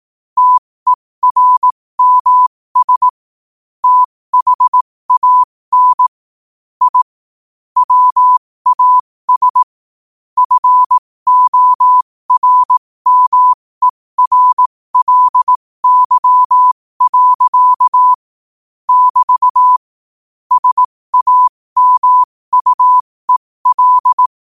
Morse code, heard as a keyed tone, about 18 wpm, spaced as for 15 wpm, 1000 Hz. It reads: TERMS THAN I WAS FORMERLY. <BT> SAMUEL
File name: qotdfiles / 15wpm